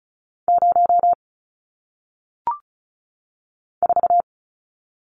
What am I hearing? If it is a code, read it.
Morse code, heard as a keyed tone, about 36 wpm, 700 Hz. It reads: 0 4
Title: Numbers - 1-Digit 35wpm